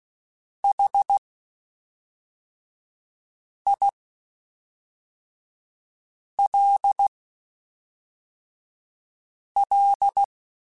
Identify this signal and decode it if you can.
Morse code, heard as a keyed tone, about 16 wpm, spaced as for 3 wpm, 800 Hz. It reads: HILL
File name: (spaced) / QSOs Spaced-mp3s / kc7haa @5 sp